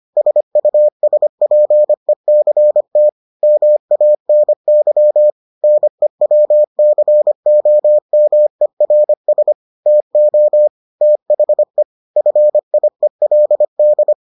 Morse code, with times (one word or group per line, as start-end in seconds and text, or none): 0.2-3.1 SUSPECT
3.4-5.3 MANY
5.6-9.5 NEWCOMERS
9.9-10.7 TO
11.0-11.8 THE
12.2-14.1 FIELD